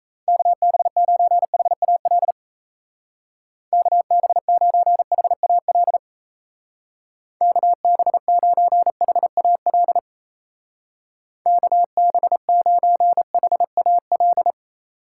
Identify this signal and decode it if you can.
Morse code, heard as a keyed tone, about 34 wpm, 700 Hz. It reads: KB9HAL KB9HAL KB9HAL KB9HAL
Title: Speed Racing - USA 2x3 Call Signs 28wpm